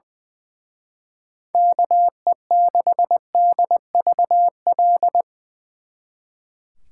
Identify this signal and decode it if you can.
Morse code, heard as a keyed tone, about 21 wpm, 700 Hz. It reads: KE6DVL
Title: Speed Racing - USA 2x3 Call Signs 20wpm